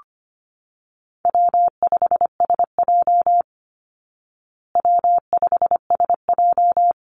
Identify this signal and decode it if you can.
Morse code, heard as a keyed tone, about 25 wpm, 700 Hz. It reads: W5SJ W5SJ
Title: Call Signs Repeated 3x - 25wpm